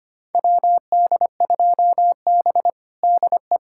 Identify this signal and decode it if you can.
Morse code, heard as a keyed tone, about 25 wpm, 700 Hz. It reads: WD2B DE
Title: USA Callsign DE Callsign - 4 Characters - 25wpm